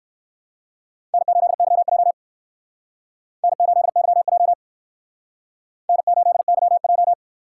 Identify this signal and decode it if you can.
Morse code, heard as a keyed tone, about 59 wpm, 700 Hz. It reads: N8YJ N8YJ N8YJ